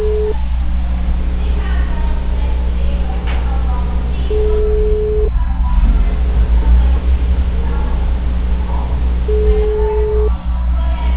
Ringback tone (Europe, 425 Hz), with noise.